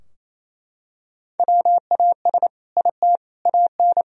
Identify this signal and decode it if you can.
Morse code, heard as a keyed tone, about 28 wpm, 700 Hz. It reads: WAS IT AN
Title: Sentences from Top 1000 Words - Review Part 4 - 28wpm